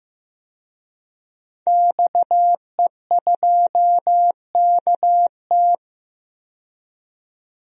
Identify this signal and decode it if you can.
Morse code, heard as a keyed tone, about 15 wpm, 700 Hz. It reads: XE2KT
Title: DXCC-100 Call Signs - 5 Characters - 15wpm